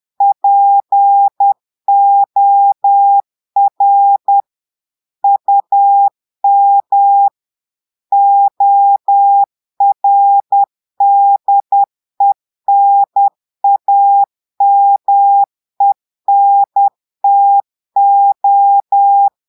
Morse code, heard as a keyed tone, about 10 wpm, 800 Hz. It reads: POR UM ORDENAMENTO
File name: Portugais - texte en clair 1 - vitesse - 10 mots minute